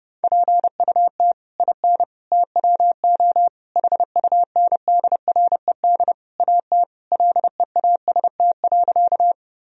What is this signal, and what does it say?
Morse code, heard as a keyed tone, about 30 wpm, 700 Hz. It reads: PUT IN TWO HUNDRED AT LEAST.
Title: Sentences from Top 1000 Words - Review Part 2 - 30wpm